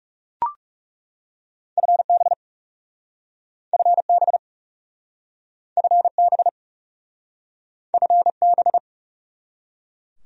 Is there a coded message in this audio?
Yes: Morse code from 1.8 to 8.8 s, heard as a keyed tone, about 40 wpm, 700 Hz. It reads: FB FB FB FB